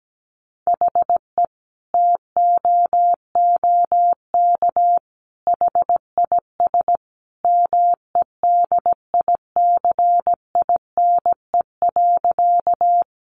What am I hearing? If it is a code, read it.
Morse code, heard as a keyed tone, about 17 wpm, 700 Hz. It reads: HE TOOK HIS MEDICINE.
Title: Sentences from Top 900 Words 17wpm